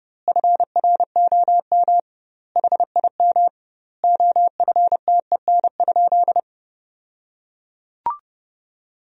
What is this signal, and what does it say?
Morse code, heard as a keyed tone, about 30 wpm, 700 Hz. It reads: FROM HIM OFTEN?
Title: 2x Word-Spacing - Sentences from Top 400 Words - 30wpm